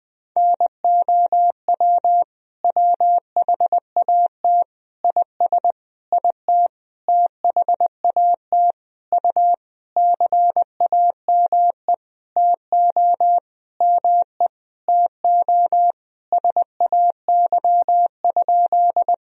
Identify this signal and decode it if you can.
Morse code, heard as a keyed tone, about 21 wpm, 700 Hz. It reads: NOW WHAT IS IT THAT U CAME TO ME TO SAY?
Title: Sentences from Top 200 Words and Abbreviations 20wpm